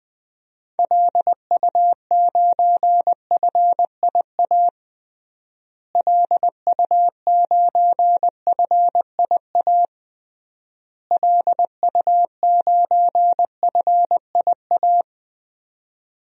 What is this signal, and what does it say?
Morse code, heard as a keyed tone, about 21 wpm, 700 Hz. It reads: LU9FIA LU9FIA LU9FIA
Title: DXCC-100 Call Signs - 6 Characters - Repeated 3x - 20wpm